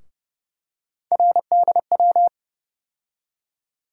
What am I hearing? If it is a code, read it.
Morse code, heard as a keyed tone, about 31 wpm, 700 Hz. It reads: RDW
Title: USA Call Sign Postfixes - 3 Letters 30wpm